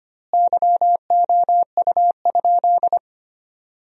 Morse code, heard as a keyed tone, about 25 wpm, 700 Hz. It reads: YOU?
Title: 2x Word-Spacing - Sentences from Top 400 Words - 25wpm